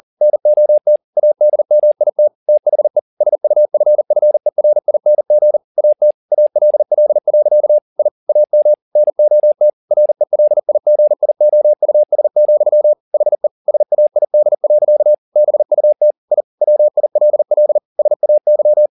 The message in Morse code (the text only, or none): NOT ADMIT THE SUFFERING AT ALL. I AM NOT RELIGIOUS, HE SAID. BUT I WILL SAY